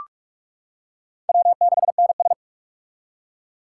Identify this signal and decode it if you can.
Morse code, heard as a keyed tone, about 42 wpm, 700 Hz. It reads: W6NS
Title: K1USN SST QSO Practice - 45wpm